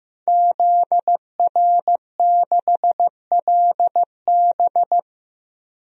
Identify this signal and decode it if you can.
Morse code, heard as a keyed tone, about 15 wpm, 700 Hz. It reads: ZR6LB
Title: DXCC-100 Call Signs - 5 Characters - 15wpm